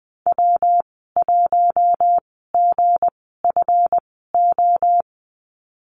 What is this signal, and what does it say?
Morse code, heard as a keyed tone, about 20 wpm, spaced as for 15 wpm, 700 Hz. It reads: W1GFO
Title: 186 Lesson 30g - Letter b - Call Signs 20wpm